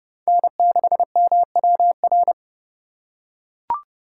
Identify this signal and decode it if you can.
Morse code, heard as a keyed tone, about 30 wpm, 700 Hz. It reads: N6MWR
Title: USA Call Signs - 5 Characters - 3x Character Spacing - 30wpm